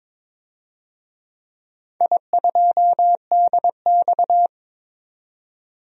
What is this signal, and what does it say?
Morse code, heard as a keyed tone, about 22 wpm, 700 Hz. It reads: I2DX
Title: DXCC-100 Call Signs - 4 Characters - 22wpm